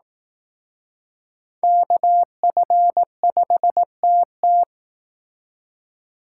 Morse code, heard as a keyed tone, about 18 wpm, 700 Hz. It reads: KF5TT
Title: Speed Racing - USA 2x2 Call Signs 15wpm